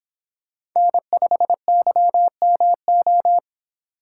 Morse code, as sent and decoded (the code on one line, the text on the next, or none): -. ..... -.-- -- ---
N5YMO